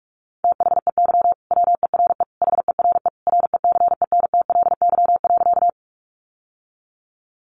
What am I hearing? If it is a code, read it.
Morse code, heard as a keyed tone, about 46 wpm, 700 Hz. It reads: THEY WERE HERE RECENTLY.
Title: Sentences from Top 1200 Words 45wpm